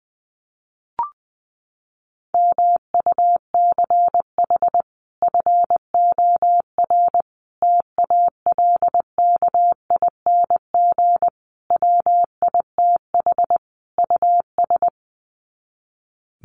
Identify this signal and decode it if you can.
Morse code, heard as a keyed tone, about 20 wpm, 700 Hz. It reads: MUCH FOR TALKING WITH US
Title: Sets of 5 Words Encore 20wpm